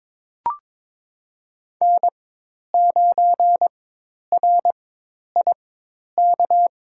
Morse code, heard as a keyed tone, about 22 wpm, spaced as for 10 wpm, 700 Hz. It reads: N9RIK
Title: USA Call Signs - 5 Characters - 4x Character Spacing - 22wpm